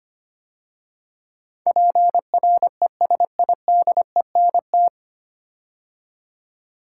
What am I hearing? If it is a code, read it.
Morse code, heard as a keyed tone, about 25 wpm, 700 Hz. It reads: PRESIDENT
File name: practice / 25wpm / 9 Letter Words - First 3 Letters Given - 25wpm